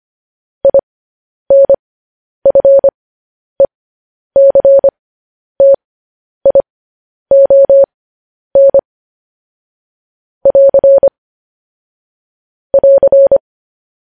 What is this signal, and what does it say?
Morse code, heard as a keyed tone, about 25 wpm, spaced as for 10 wpm, 550 Hz. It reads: INFECTION <AR> <AR>
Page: news headlines in morse code at 10 wpm